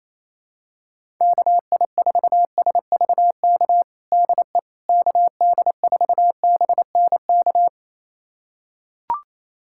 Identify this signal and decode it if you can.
Morse code, heard as a keyed tone, about 28 wpm, 700 Hz. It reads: KI4SVK DE KD4BNK
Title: USA Callsign DE Callsign - 6 Characters - 28wpm